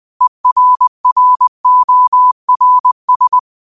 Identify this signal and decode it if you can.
Morse code, heard as a keyed tone, about 20 wpm, 1000 Hz. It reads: ERRORS
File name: qotdfiles / 20wpm